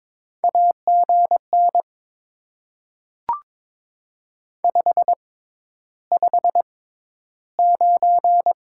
Morse code, heard as a keyed tone, about 22 wpm, spaced as for 11 wpm, 700 Hz. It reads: AGN 559
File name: Top 90 QSO Elements - 6x Character Spacing - 22wpm